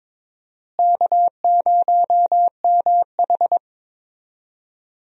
Morse code, heard as a keyed tone, about 22 wpm, 700 Hz. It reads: K0MH
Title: USA Call Signs - 4 Characters - Repeated 3x - 22wpm